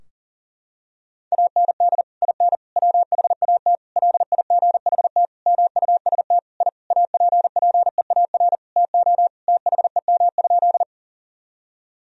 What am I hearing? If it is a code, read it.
Morse code, heard as a keyed tone, about 37 wpm, 700 Hz. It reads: AND IN WHAT LIGHT MUST I APPEAR TO THEM?
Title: Sentences from Top 1000 Words - Review Part 2 - 40wpm